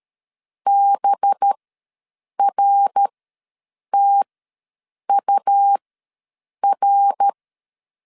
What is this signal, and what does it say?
Morse code, heard as a keyed tone, about 13 wpm, spaced as for 7 wpm, 800 Hz. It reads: BRTUR
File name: 5 LB ws 12wpm